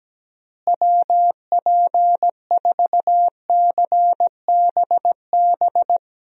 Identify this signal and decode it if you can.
Morse code, heard as a keyed tone, about 17 wpm, 700 Hz. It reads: WP4CBB
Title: USA Call Signs - 6 Characters - Repeated 3x - 17wpm